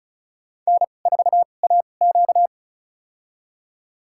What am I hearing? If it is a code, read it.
Morse code, heard as a keyed tone, about 36 wpm, spaced as for 25 wpm, 700 Hz. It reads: N4AQ
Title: USA Call Signs - 2x Character Spacing - 35wpm